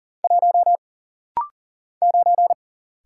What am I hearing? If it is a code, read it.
Morse code, heard as a keyed tone, about 40 wpm, 700 Hz. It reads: 1 9
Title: Single Letter-Number - Rapid-Fire 40wpm